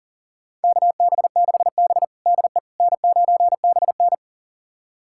Morse code, heard as a keyed tone, about 40 wpm, 700 Hz. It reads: KB6B DE N9BN
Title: USA Callsign DE Callsign - 4 Characters - 40wpm